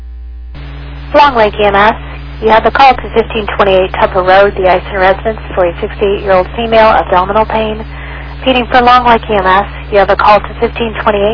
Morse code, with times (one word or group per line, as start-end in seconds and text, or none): none